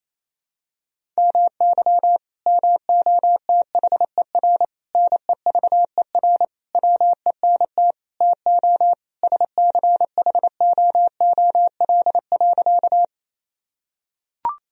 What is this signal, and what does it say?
Morse code, heard as a keyed tone, about 28 wpm, 700 Hz. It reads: MY MOTHER NEVER WENT TO SCHOOL.
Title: Sentences from Top 1000 Words - Review Part 1 - 28wpm